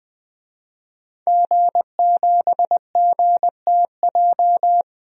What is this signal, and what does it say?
Morse code, heard as a keyed tone, about 20 wpm, 700 Hz. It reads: G7GTJ
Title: DXCC-100 Call Signs - 5 Characters - 20wpm